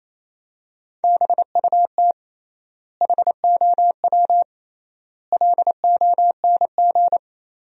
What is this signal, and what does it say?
Morse code, heard as a keyed tone, about 28 wpm, 700 Hz. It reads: BUT HOW LONG
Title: 3x Word-Spacing - Sentences from Top 500 Words - 28wpm